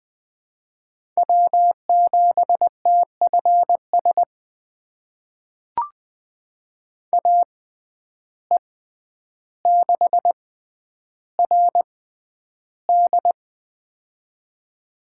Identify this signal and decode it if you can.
Morse code, heard as a keyed tone, about 20 wpm, spaced as for 10 wpm, 700 Hz. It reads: W7TFS AE6RD
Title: USA Call Signs - 5 Characters - 6x Character Spacing - 20wpm